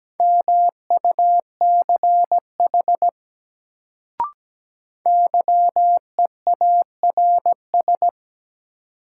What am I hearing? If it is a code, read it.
Morse code, heard as a keyed tone, about 17 wpm, 700 Hz. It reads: MUCH YEARS
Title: Sentences from Top 200 Words - New Word Drill - 17wpm